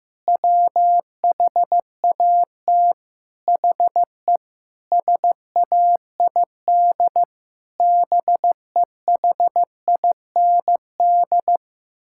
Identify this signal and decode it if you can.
Morse code, heard as a keyed tone, about 15 wpm, 700 Hz. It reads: WHAT HE SAID BEHIND
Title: Sentences from Top 1000 Words - Review Part 6 - 15wpm